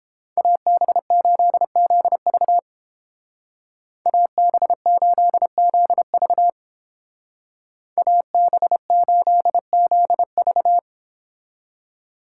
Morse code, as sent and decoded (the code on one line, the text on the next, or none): .- -... ---.. --.. ...- / .- -... ---.. --.. ...- / .- -... ---.. --.. ...-
AB8ZV AB8ZV AB8ZV